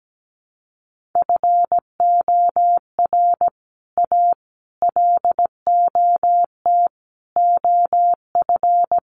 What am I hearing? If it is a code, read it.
Morse code, heard as a keyed tone, about 17 wpm, 700 Hz. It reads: FOR A LOT OF
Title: Sets of 5 Words Encore 17wpm